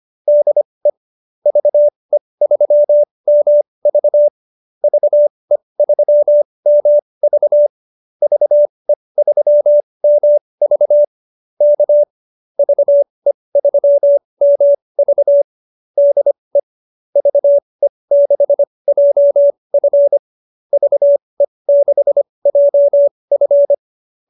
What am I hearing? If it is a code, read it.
Morse code, heard as a keyed tone, about 25 wpm, spaced as for 20 wpm, 590 Hz. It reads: DE VE3MV VE3MV VE3MV K VE3MV DE VE6JF VE6JF
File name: QSO303 20